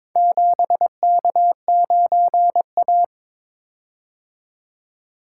Morse code, heard as a keyed tone, about 23 wpm, 700 Hz. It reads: 7K9A
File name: practice / 22wpm / Four Letter-Number Combinations - 22wpm